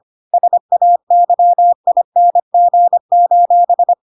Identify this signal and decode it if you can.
Morse code, heard as a keyed tone, about 25 wpm, 700 Hz. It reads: SAYING:
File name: Pinocchio / Pinocchio ch4 25 25 ik1hge